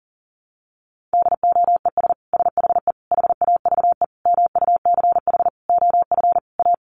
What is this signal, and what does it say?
Morse code, heard as a keyed tone, about 37 wpm, 700 Hz. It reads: DOES SHE HAVE MUCH OF A
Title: Sentences from Top 1400 Words 40wpm